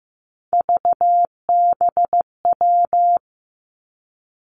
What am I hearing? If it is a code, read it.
Morse code, heard as a keyed tone, about 15 wpm, 700 Hz. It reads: VBW
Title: Three Challenging Letter-Number Combinations 15wpm